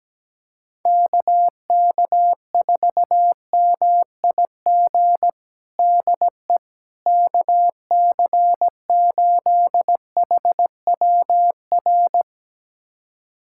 Morse code, heard as a keyed tone, about 17 wpm, 700 Hz. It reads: KK4MIG DE KC8HWR
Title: USA Callsign DE Callsign 17wpm